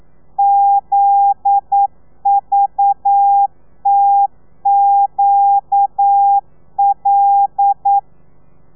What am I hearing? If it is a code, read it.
Morse code, heard as a keyed tone, about 9 wpm, 800 Hz. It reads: ZVTQL
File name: telegrafia / i2tfj / Lez9 40